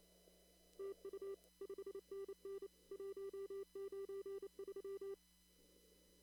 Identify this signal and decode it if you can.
Morse code, heard as a keyed tone, about 26 wpm, 390 Hz. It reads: TU 5NN 193
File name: CW / kt5j wpx cw-op-k5pi